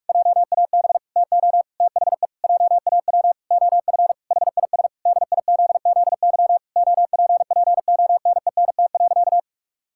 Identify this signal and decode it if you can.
Morse code, heard as a keyed tone, about 46 wpm, 700 Hz. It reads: JAB TO THE JAW OF HIS DIZZY OPPONENT.
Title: Pangram Challenge v2 - 45wpm